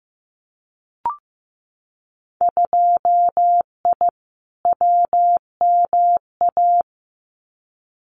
Morse code, heard as a keyed tone, about 15 wpm, 700 Hz. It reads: 2I WMA
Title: Winter Field Day Exchanges Encore 15wpm